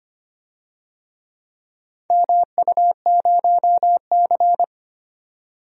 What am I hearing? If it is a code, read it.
Morse code, heard as a keyed tone, about 25 wpm, 700 Hz. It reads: MU0C
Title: DXCC-100 Call Signs - 4 Characters - 25wpm